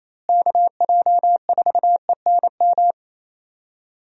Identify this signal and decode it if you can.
Morse code, heard as a keyed tone, about 28 wpm, 700 Hz. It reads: KJ4ENM
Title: USA Call Signs - 6 Characters - Repeated 2x - 28wpm